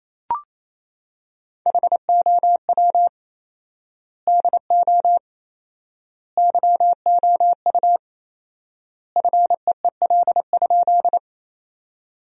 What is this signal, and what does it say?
Morse code, heard as a keyed tone, about 28 wpm, 700 Hz. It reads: HOW DO YOU FEEL?
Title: 4x Word-Spacing - Sentences from Top 400 Words - 28wpm